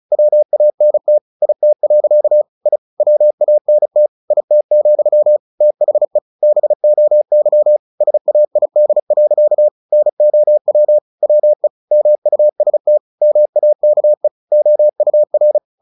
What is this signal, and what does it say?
Morse code, heard as a keyed tone, about 34 wpm, 600 Hz. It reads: WANT IT. I WANT IT, THE BOY SAID. NOW WE MUST MAKE OUR
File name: the old man and the sea 2 35wpm